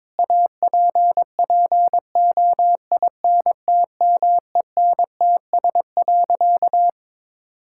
Morse code, heard as a keyed tone, about 22 wpm, 700 Hz. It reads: APPOINTMENTS.
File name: Difficult Sentences with Hints - 22wpm